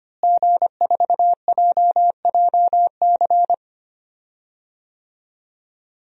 Morse code, heard as a keyed tone, about 25 wpm, 700 Hz. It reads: G4JJC
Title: DXCC-100 Call Signs - 5 Characters - Repeated 2x - 25wpm